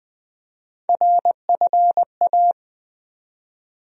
Morse code, heard as a keyed tone, about 21 wpm, 700 Hz. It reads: RFA
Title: USA Call Sign Postfixes - 3 Letters 20wpm